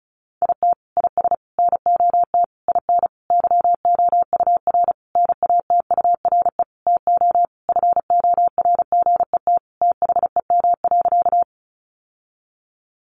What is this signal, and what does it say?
Morse code, heard as a keyed tone, about 34 wpm, 700 Hz. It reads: IT IS NOT IN YOUR NATURE TO FORGET THEM.
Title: Sentences from Top 1200 Words 35wpm